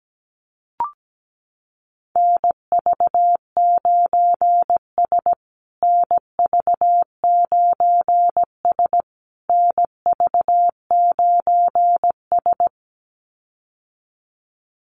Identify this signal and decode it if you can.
Morse code, heard as a keyed tone, about 17 wpm, 700 Hz. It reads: NV9S NV9S NV9S
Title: USA Call Signs Repeated 3x 17wpm